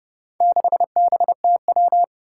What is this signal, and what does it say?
Morse code, heard as a keyed tone, about 30 wpm, 700 Hz. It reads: 6BTW
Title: Four Challenging Letter-Number Combinations 30wpm